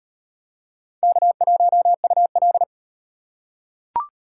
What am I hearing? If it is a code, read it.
Morse code, heard as a keyed tone, about 36 wpm, 700 Hz. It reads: K1UL